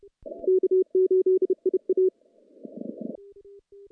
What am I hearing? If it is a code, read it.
Morse code, heard as a keyed tone, about 30 wpm, 370 Hz. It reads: K8IA
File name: contests / CW-SS-k5tr- -1400